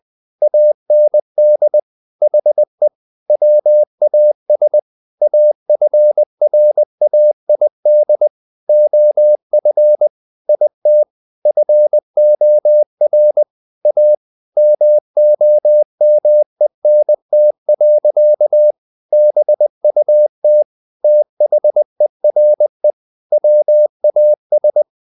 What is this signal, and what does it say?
Morse code, heard as a keyed tone, about 20 wpm, 600 Hz. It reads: AND HE WAS AFRAID OF IT FOR A MOMENT. BUT THERE WAS
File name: the old man and the sea part 2 20wpm